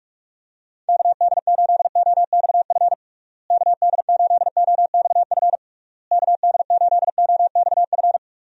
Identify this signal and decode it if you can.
Morse code, heard as a keyed tone, about 46 wpm, 700 Hz. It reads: KD8OXF KD8OXF KD8OXF
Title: USA Call Signs - 6 Characters - Repeated 3x - 45wpm